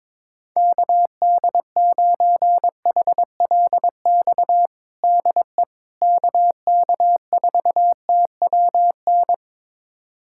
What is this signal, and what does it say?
Morse code, heard as a keyed tone, about 22 wpm, 700 Hz. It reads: KD9HLX DE KK4TWN
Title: USA Callsign DE Callsign - 6 Characters - 22wpm